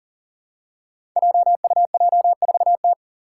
Morse code, heard as a keyed tone, about 40 wpm, 700 Hz. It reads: JUJ4T
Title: Five Challenging Letter-Number Combinations 40wpm